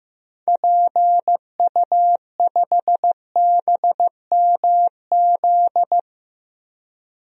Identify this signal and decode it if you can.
Morse code, heard as a keyed tone, about 15 wpm, 700 Hz. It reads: PU5BMZ
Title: DXCC-100 Call Signs - 6 Characters - Repeated 3x - 15wpm